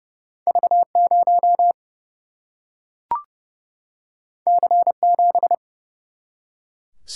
Morse code, heard as a keyed tone, about 29 wpm, 700 Hz. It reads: V0 C7
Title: Single-Letter Single-Number 30wpm